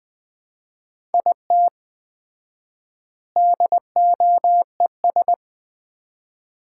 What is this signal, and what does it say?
Morse code, heard as a keyed tone, about 20 wpm, 700 Hz. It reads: IT DOES
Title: 4x Word-Spacing - Sets of 3 Words - 20wpm